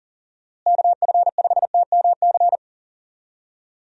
Morse code, heard as a keyed tone, about 40 wpm, 700 Hz. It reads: KF5TMC
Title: USA Call Signs - 6 Characters - Repeated 3x - 40wpm